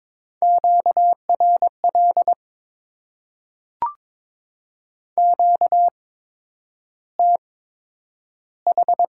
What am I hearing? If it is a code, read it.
Morse code, heard as a keyed tone, about 22 wpm, spaced as for 9 wpm, 700 Hz. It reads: QRL QTH
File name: Top 20 QSO Elements - 8x Character Spacing - 22wpm